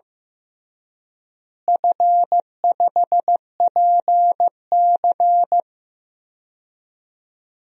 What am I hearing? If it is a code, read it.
Morse code, heard as a keyed tone, about 16 wpm, 700 Hz. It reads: F5PC
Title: DXCC-100 Call Signs - 4 Characters - Repeated 3x - 15wpm